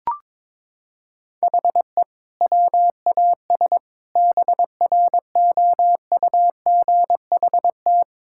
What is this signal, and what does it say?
Morse code, heard as a keyed tone, about 22 wpm, 700 Hz. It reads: HE WAS BROUGHT
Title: Sentences from Top 1600 Words 22wpm